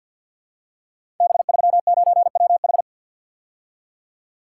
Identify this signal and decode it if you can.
Morse code, heard as a keyed tone, about 48 wpm, 700 Hz. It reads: B39WH
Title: Code Groups - 5 Letter:Numbers - 50wpm